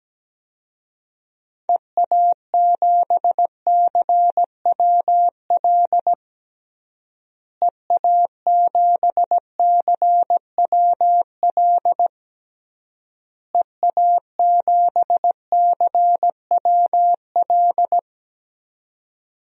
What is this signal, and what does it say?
Morse code, heard as a keyed tone, about 17 wpm, 700 Hz. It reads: EA7CWL EA7CWL EA7CWL
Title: DXCC-100 Call Signs - 6 Characters - Repeated 3x - 17wpm